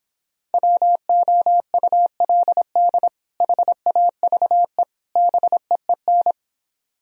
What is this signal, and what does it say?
Morse code, heard as a keyed tone, about 29 wpm, 700 Hz. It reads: WOULD HAVE BEEN